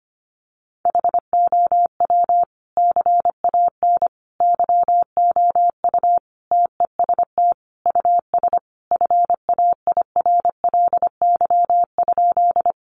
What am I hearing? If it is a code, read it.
Morse code, heard as a keyed tone, about 25 wpm, 700 Hz. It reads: HOW CAN YOU TEST US FAIRLY?
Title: Sentences from Top 1100 Words 25wpm